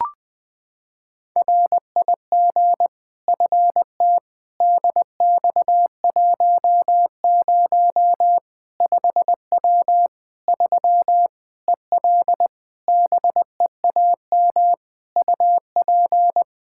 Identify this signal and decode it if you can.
Morse code, heard as a keyed tone, about 20 wpm, 700 Hz. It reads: RIG FT DX10 5W 3 EL BEAM UP
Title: Practice QSO - Exchange 2 - Info - 20wpm